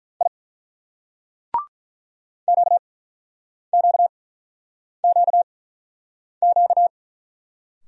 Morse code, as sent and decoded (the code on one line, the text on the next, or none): .. / --.- --.- --.- --.-
I QQQQ